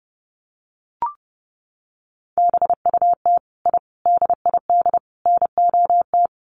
Morse code, heard as a keyed tone, about 30 wpm, 700 Hz. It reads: BUT I DID NOT
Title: Sets of 4 Words 30wpm